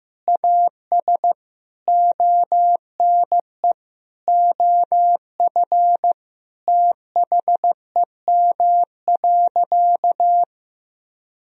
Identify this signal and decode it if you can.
Morse code, heard as a keyed tone, about 15 wpm, 700 Hz. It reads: AS ONE OF THEM.
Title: Sentences from Top 1000 Words - Review Part 2 - 15wpm